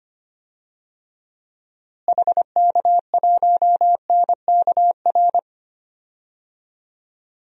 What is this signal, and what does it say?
Morse code, heard as a keyed tone, about 25 wpm, 700 Hz. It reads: HK1NKR
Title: DXCC-100 Call Signs - 6 Characters - Repeated 2x - 25wpm